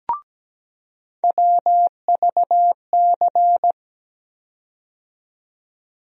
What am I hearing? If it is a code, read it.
Morse code, heard as a keyed tone, about 17 wpm, 700 Hz. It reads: WVC
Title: USA Call Sign Postfixes - 3 Letters 17wpm